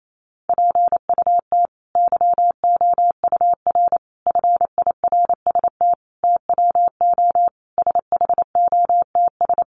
Morse code, heard as a keyed tone, about 28 wpm, 700 Hz. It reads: PUT YOUR FIRST TWO SHOTS
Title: A Sound Of Thunder 28wpm